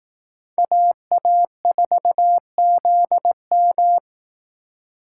Morse code, heard as a keyed tone, about 19 wpm, 700 Hz. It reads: AA4ZM
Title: Speed Racing - USA Call Signs 15wpm